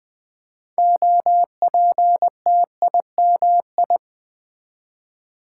Morse code, heard as a keyed tone, about 20 wpm, 700 Hz. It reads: OPTIMI
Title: Build A Word - 3 Letters At A Time - 8-Letter Words - 20wpm